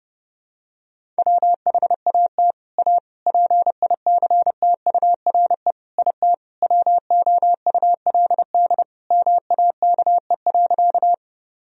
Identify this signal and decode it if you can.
Morse code, heard as a keyed tone, about 30 wpm, 700 Hz. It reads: WHAT A PICTURE IT WOULD MAKE.
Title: Sentences from Top 1000 Words - Review Part 1 - 30wpm